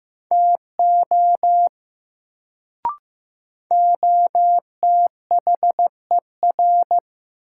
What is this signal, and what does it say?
Morse code, heard as a keyed tone, about 15 wpm, 700 Hz. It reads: TO OTHER
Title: Sentences from Top 100 Words - New Word Drill - 15wpm